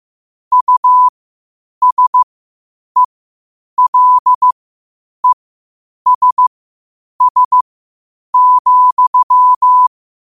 Morse code, heard as a keyed tone, about 15 wpm, spaced as for 8 wpm, 1000 Hz. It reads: USELESS,
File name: qotdfiles / 8wpm